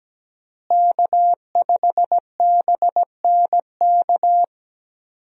Morse code, heard as a keyed tone, about 17 wpm, 700 Hz. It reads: K5BNK
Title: USA Call Signs - 5 Characters - Repeated 3x - 17wpm